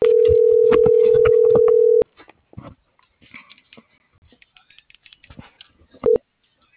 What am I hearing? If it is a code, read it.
Ringback tone (North America), 440 + 480 Hz.